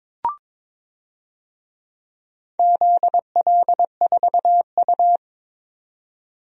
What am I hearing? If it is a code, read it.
Morse code, heard as a keyed tone, about 22 wpm, 700 Hz. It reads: ZL4U
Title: DXCC-100 Call Signs - 4 Characters - 22wpm